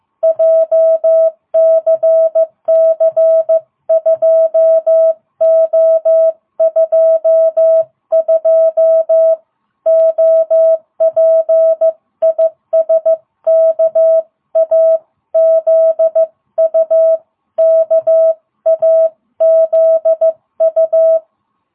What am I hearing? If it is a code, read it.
Morse code, heard as a keyed tone, about 15 wpm, 660 Hz. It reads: JCC2O22 OP IS KAZU KAZU